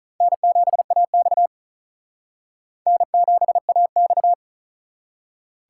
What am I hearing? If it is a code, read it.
Morse code, heard as a keyed tone, about 39 wpm, 700 Hz. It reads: N7AX N7AX
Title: Speed Racing - USA 1x2 Call Signs 35wpm